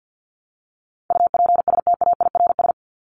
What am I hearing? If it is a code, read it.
Morse code, heard as a keyed tone, about 48 wpm, 700 Hz. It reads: UPSTAIRS
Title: Top 1751-2000 Words 50wpm